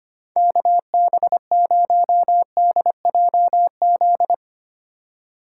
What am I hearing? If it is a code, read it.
Morse code, heard as a keyed tone, about 25 wpm, 700 Hz. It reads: KB0DJZ
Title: USA Call Signs - 6 Characters - 2x Character Spacing - 25wpm